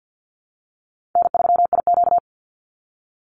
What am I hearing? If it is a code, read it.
Morse code, heard as a keyed tone, about 50 wpm, 700 Hz. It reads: N3IQ
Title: Call Signs - 50wpm